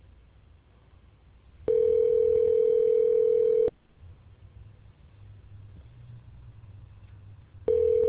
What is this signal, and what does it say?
Ringback tone (North America), 440 + 480 Hz.